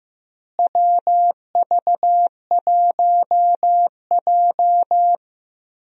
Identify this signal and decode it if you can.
Morse code, heard as a keyed tone, about 15 wpm, 700 Hz. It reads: WV1J
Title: USA Call Signs - 4 Characters - Repeated 2x - 15wpm